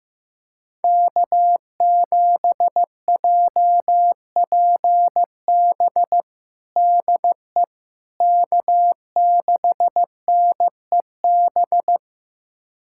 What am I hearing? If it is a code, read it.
Morse code, heard as a keyed tone, about 15 wpm, 700 Hz. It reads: K7JPB DE K6NEB
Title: USA Callsign DE Callsign 15wpm